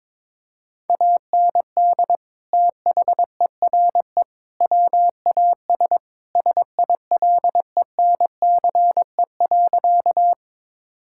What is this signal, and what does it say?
Morse code, heard as a keyed tone, about 22 wpm, 700 Hz. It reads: AND THERE WAS SILENCE.
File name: practice / 22wpm / Sentences with 3-Word Phrases - 22wpm